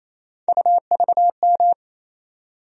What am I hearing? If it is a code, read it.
Morse code, heard as a keyed tone, about 28 wpm, 700 Hz. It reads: UVM
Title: USA Call Sign Postfixes - 3 Letters 28wpm